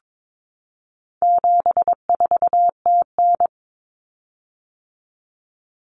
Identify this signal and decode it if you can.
Morse code, heard as a keyed tone, about 22 wpm, 700 Hz. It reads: 74TN
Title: Common Cut Numbers - 4-Digit 22wpm